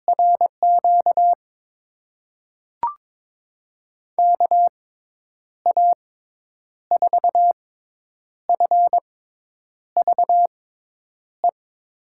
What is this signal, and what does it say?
Morse code, heard as a keyed tone, about 22 wpm, spaced as for 9 wpm, 700 Hz. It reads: RQ KA4FVE
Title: USA Call Signs - 6 Characters - 6x Character Spacing - 22wpm